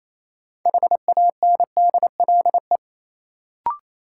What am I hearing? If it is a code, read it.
Morse code, heard as a keyed tone, about 28 wpm, 700 Hz. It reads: HANDLE
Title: Sentences from Top 1100 Words - New Word Drill - 28wpm